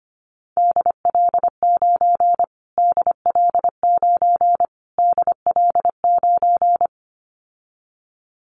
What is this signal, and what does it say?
Morse code, heard as a keyed tone, about 25 wpm, 700 Hz. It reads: DL9 DL9 DL9
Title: Callsign Prefixes for Top 25 DXCC - Repeated 3x - 25wpm